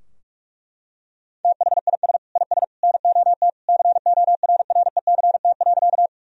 Morse code, heard as a keyed tone, about 41 wpm, 700 Hz. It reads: THIS IS NOT CORRECT.
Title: Sentences from Top 1000 Words - Review Part 2 - 45wpm